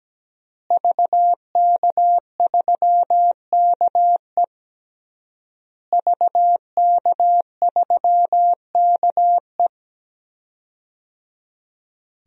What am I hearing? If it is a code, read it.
Morse code, heard as a keyed tone, about 17 wpm, 700 Hz. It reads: VK3KE VK3KE
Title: DXCC-100 Call Signs - 5 Characters - Repeated 3x - 17wpm